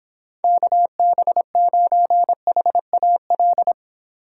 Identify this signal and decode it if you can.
Morse code, heard as a keyed tone, about 26 wpm, 700 Hz. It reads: KB9HAL